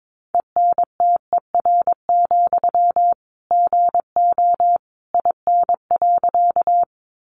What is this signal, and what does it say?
Morse code, heard as a keyed tone, about 22 wpm, 700 Hz. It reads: ENTER, GO IN.
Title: Sentences from Top 1300 Words 22wpm